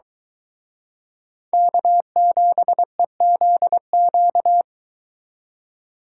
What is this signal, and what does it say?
Morse code, heard as a keyed tone, about 23 wpm, 700 Hz. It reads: K7EZQ